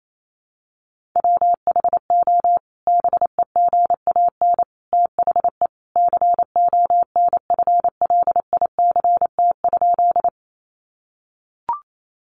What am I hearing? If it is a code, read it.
Morse code, heard as a keyed tone, about 28 wpm, 700 Hz. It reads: WHO BEGAN THE CONFLICT?
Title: Sentences from Top 1300 Words 28wpm